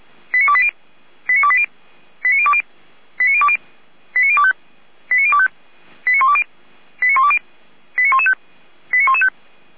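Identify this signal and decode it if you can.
Five-tone selective call (ZVEI), heard as quick runs of five tones, tones of about 70 ms. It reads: 89280 89280 89020 89020 89025 89025 89120 89120 89115 89115
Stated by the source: ZVEI five-tone selective call received by SDR, logged by feuerwehr-orlamuende.org (fire brigade Orlamünde).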